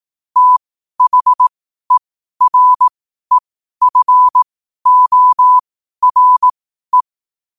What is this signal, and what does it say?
Morse code, heard as a keyed tone, about 18 wpm, spaced as for 12 wpm, 1000 Hz. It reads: THEREFORE
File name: qotdfiles / 12wpm